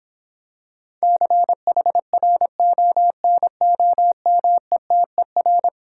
Morse code, heard as a keyed tone, about 26 wpm, 700 Hz. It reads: CHRONOMETER